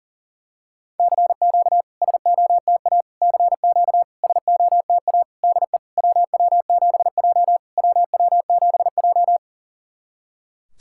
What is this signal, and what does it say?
Morse code, heard as a keyed tone, about 40 wpm, 700 Hz. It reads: CQ SOTA CQ SOTA DE WW7J WW7J
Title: SOTA QSO Practice 40wpm